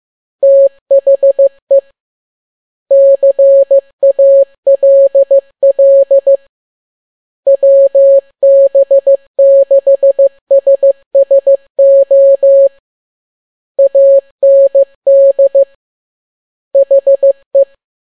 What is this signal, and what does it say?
Morse code, heard as a keyed tone, about 15 wpm, 550 Hz. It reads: THE CALL WB6SSO AND HE
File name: n6tt p1 15C13